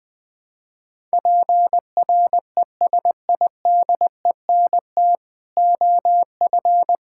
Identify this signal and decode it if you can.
Morse code, heard as a keyed tone, about 21 wpm, 700 Hz. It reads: PRESIDENT OF
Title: Build A Phrase - 5-Word Phrases - 20wpm